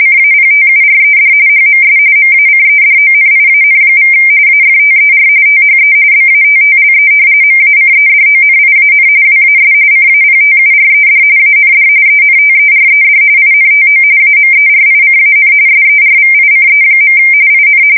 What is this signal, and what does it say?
RTTY, about 45.45 baud. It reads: CAPTURED HMS MACEDONIAN= AFTER THE BATTLE MACEDONIAN WAS CAPTURED AND ENTERED INTO AMERICAN SERVICE. IN
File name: fsk170 / usnavy2